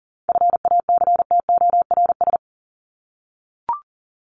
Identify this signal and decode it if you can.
Morse code, heard as a keyed tone, about 40 wpm, 700 Hz. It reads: FACTORS
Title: ES:S Suffix - Top 100 Words with Base Words 40wpm